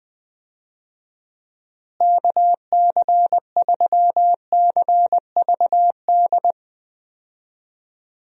Morse code, heard as a keyed tone, about 21 wpm, 700 Hz. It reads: KC3CVD
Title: USA Call Signs - 6 Characters - Repeated 3x - 20wpm